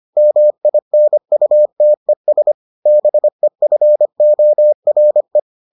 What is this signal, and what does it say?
Morse code, heard as a keyed tone, about 25 wpm, 600 Hz. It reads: MINUTES BEFORE